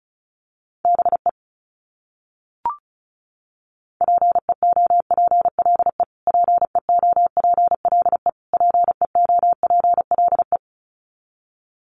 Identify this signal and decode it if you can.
Morse code, heard as a keyed tone, about 34 wpm, 700 Hz. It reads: BE PEOPLE PEOPLE PEOPLE
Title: Top 300 Words Repeated 35wpm